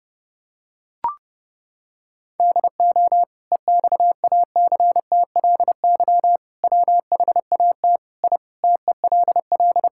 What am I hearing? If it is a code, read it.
Morse code, heard as a keyed tone, about 30 wpm, 700 Hz. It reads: DO EXACTLY WHAT I TELL
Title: Sentences from Top 1000 Words - Review Part 6 - 30wpm